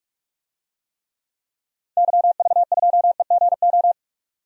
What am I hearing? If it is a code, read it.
Morse code, heard as a keyed tone, about 41 wpm, 700 Hz. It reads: YV2EGO
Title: DXCC-100 Call Signs - 6 Characters - 45wpm